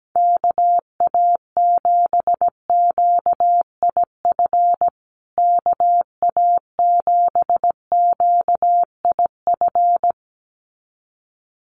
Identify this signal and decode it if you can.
Morse code, heard as a keyed tone, about 17 wpm, 700 Hz. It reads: KA7QIF KA7QIF
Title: USA Call Signs Repeated 2x 17wpm